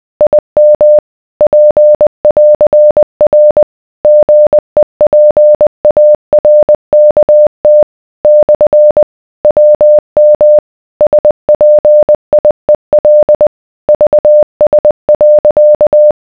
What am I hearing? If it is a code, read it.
Morse code, heard as a keyed tone, about 20 wpm, 600 Hz. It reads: IM P<AR>R GEPARKT / WM SPIEL VS.